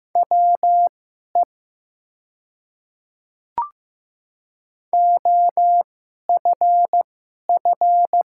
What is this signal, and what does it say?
Morse code, heard as a keyed tone, about 15 wpm, spaced as for 11 wpm, 700 Hz. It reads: WE OFF
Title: 2x Character Spacing - Top 300 Words - 15wpm